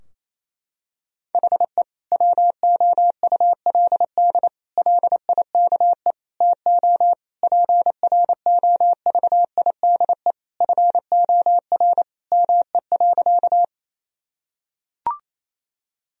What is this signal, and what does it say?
Morse code, heard as a keyed tone, about 28 wpm, 700 Hz. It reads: HE WOULD LIKE TO PROVIDE FOR ME.
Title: Sentences from Top 1000 Words - Review Part 3 - 28wpm